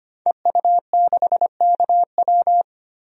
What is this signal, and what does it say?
Morse code, heard as a keyed tone, about 25 wpm, 690 Hz. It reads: EU6KW
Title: DXCC-100 Call Signs - 5 Characters - 25wpm